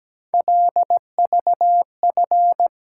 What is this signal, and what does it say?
Morse code, heard as a keyed tone, about 17 wpm, 700 Hz. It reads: LVF
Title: Code Groups - 5 Letters - 17wpm